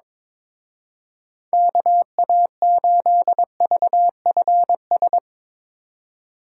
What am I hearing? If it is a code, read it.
Morse code, heard as a keyed tone, about 23 wpm, 700 Hz. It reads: KA8VFS
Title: Speed Racing - USA 2x3 Call Signs 22wpm